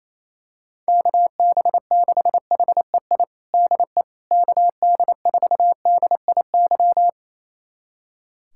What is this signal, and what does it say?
Morse code, heard as a keyed tone, about 28 wpm, 700 Hz. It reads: KB6HEI DE KD4DIY
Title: USA Callsign DE Callsign - 6 Characters - 28wpm